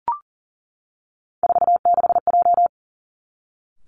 Morse code, heard as a keyed tone, about 41 wpm, 700 Hz. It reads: 46J
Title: Three Challenging Letter-Number Combinations 40wpm